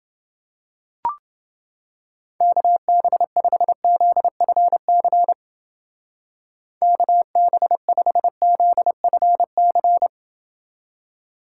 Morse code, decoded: KB5ZFC KB5ZFC

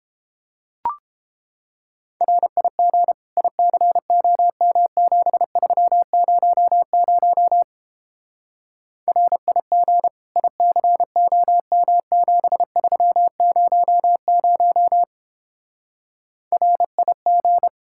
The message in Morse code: RIG ICOM7300 RIG ICOM7300 RIG

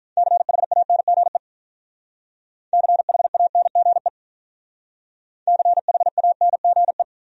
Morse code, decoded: CHANGE CHANGE CHANGE

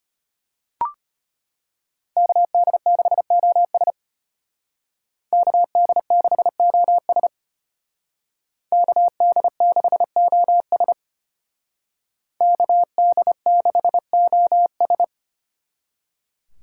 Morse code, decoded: KD6OS KD6OS KD6OS KD6OS

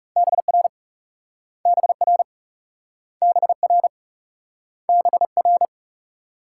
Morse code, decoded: BR BR BR BR

